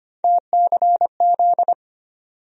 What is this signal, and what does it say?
Morse code, heard as a keyed tone, about 25 wpm, 700 Hz. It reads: TCZ
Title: USA Call Sign Postfixes - 3 Letters 25wpm